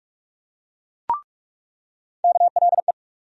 Morse code, heard as a keyed tone, about 46 wpm, 700 Hz. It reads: KLE
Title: USA Call Sign Postfixes - 3 Letters 45wpm